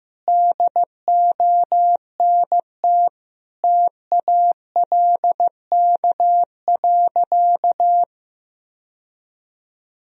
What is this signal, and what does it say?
Morse code, heard as a keyed tone, about 16 wpm, 700 Hz. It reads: DONT TALK.
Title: Sentences from Top 1000 Words - Review Part 2 - 15wpm